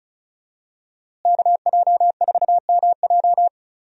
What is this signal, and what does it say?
Morse code, heard as a keyed tone, about 36 wpm, 700 Hz. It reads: KJ4MJ
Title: USA Call Signs - 5 Characters - Repeated 2x - 35wpm